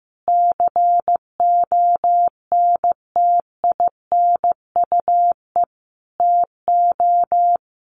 Morse code, heard as a keyed tone, about 14 wpm, 700 Hz. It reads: CONTINUE TO
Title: Sentences from Top 1400 Words 15wpm